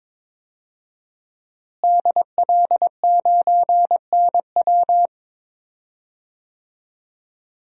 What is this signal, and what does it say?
Morse code, heard as a keyed tone, about 22 wpm, 700 Hz. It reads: DL9NW
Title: DXCC-100 Call Signs - 5 Characters - Repeated 2x - 22wpm